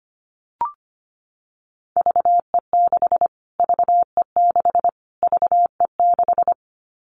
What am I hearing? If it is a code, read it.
Morse code, heard as a keyed tone, about 25 wpm, 700 Hz. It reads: VE6 VE6 VE6
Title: Callsign Prefixes for Top 25 DXCC - Repeated 3x - 25wpm